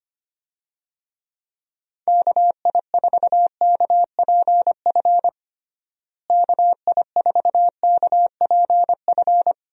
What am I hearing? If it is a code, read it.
Morse code, heard as a keyed tone, about 25 wpm, 700 Hz. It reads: KI4KPF KI4KPF
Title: USA Call Signs - 6 Characters - Repeated 3x - 25wpm